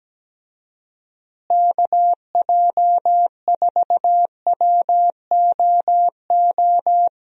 Morse code, heard as a keyed tone, about 17 wpm, 700 Hz. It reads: KJ4WOO
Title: USA Call Signs - 6 Characters - Repeated 3x - 17wpm